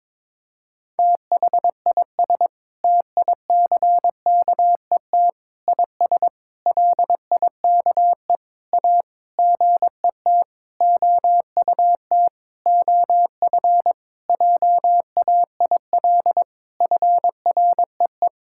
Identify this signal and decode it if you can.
Morse code, heard as a keyed tone, about 22 wpm, 700 Hz. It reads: THIS TICKET IS LIKE A GET OUT OF JAIL FREE
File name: Sentences with 7-Word Phrases - 22wpm